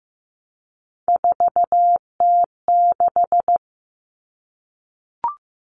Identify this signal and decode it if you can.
Morse code, heard as a keyed tone, about 15 wpm, 700 Hz. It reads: 4T6
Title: All Cut Numbers - 3-Digit 15wpm